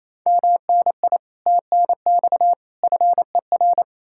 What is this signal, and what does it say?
Morse code, heard as a keyed tone, about 38 wpm, spaced as for 33 wpm, 700 Hz. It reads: MNI TNX FER
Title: Speed-Racing Common QSO Phrases - 28wpm